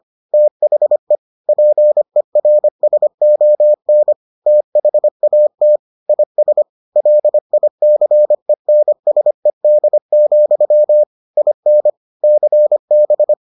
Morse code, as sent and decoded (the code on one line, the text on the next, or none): - .... / .--. .-. ... --- -. / - .... .- - / .. ... / .-.. .. -.-. -. ... -.. --..-- / .. -. / -.-. -...
THE PERSON THAT IS LICENSED, IN CB